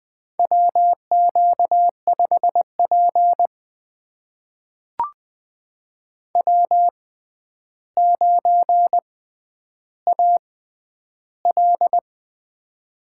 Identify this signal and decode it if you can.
Morse code, heard as a keyed tone, about 20 wpm, spaced as for 10 wpm, 700 Hz. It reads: WQ5P W9AL
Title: USA Call Signs - 4 Characters - 6x Character Spacing - 20wpm